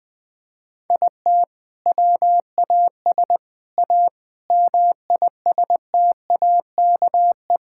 Morse code, heard as a keyed tone, about 20 wpm, 700 Hz. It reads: IT WAS A MISTAKE
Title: Sentences with 4-Word Phrases - 20wpm